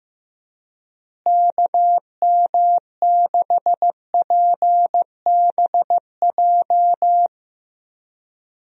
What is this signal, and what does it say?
Morse code, heard as a keyed tone, about 15 wpm, 700 Hz. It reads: KM6PBJ
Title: USA Call Signs - 6 Characters - Repeated 2x - 15wpm